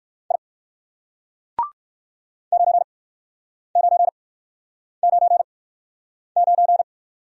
Morse code, heard as a keyed tone, about 56 wpm, 700 Hz. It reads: I 9 9 9 9
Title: Speed-Racing Single Letter-Number 45wpm